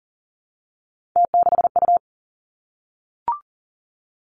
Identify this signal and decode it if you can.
Morse code, heard as a keyed tone, about 38 wpm, 700 Hz. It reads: T6U
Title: Three Challenging Letter-Number Combinations 40wpm